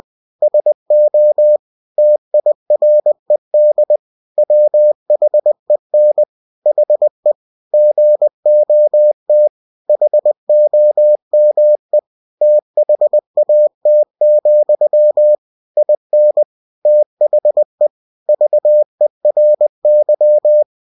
Morse code, heard as a keyed tone, about 20 wpm, 600 Hz. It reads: SO TIRED WHEN HE GOT HOME THAT, IN THE VERY
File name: cw-books / winnie-the-pooh / winnie-20wpm-page-0093